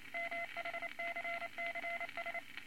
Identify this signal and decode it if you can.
Morse code, heard as a keyed tone, about 28 wpm, 2050 Hz, with radio noise. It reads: MHCCS
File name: Morse Code Records / 22 A LCM 172 Twenty Words Per Minute